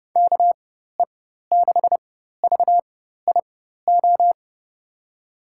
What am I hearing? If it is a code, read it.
Morse code, heard as a keyed tone, about 30 wpm, spaced as for 14 wpm, 700 Hz. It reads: KE6VIO
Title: USA Call Signs - 4x Character Spacing - 30wpm